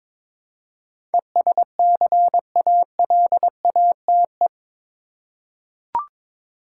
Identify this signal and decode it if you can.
Morse code, heard as a keyed tone, about 22 wpm, 700 Hz. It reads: ESCALATE
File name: practice / 22wpm / Eight Letter Words - 22wpm